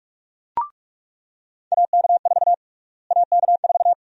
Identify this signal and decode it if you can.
Morse code, heard as a keyed tone, about 46 wpm, 700 Hz. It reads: AK4 AK4
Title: USA Call Sign Prefixes - Repeated 2x - 45wpm